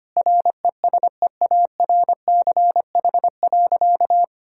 Morse code, heard as a keyed tone, about 25 wpm, 700 Hz. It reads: RESEARCH.
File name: practice / 25wpm / Sentences with 9-Word Phrases - 25wpm